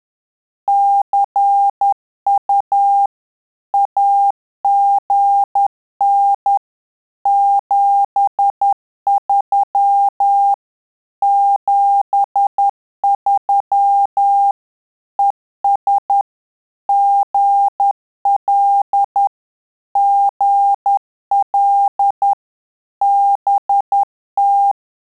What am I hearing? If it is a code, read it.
Morse code, heard as a keyed tone, about 10 wpm, 800 Hz. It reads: CU AGN 73 73 ES GL GL BT
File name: Telegraphy-group / QSO CW 50 CPM